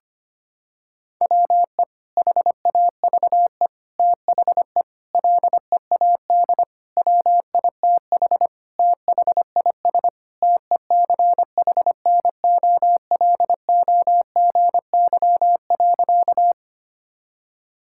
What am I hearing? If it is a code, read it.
Morse code, heard as a keyed tone, about 25 wpm, 700 Hz. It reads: WE HAVE THE LEAD WITH THIS TECHNOLOGY.
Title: Sentences from Top 1000 Words - Review Part 5 - 25wpm